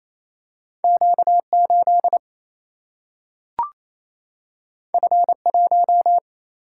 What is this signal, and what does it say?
Morse code, heard as a keyed tone, about 28 wpm, 700 Hz. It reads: Q8 F1
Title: Single-Letter Single-Number 28wpm